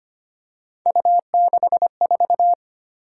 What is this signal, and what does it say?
Morse code, heard as a keyed tone, about 25 wpm, 700 Hz. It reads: U64
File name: Three Letter-Number Combinations - 25wpm